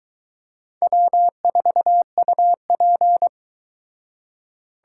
Morse code, heard as a keyed tone, about 22 wpm, 700 Hz. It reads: W4UP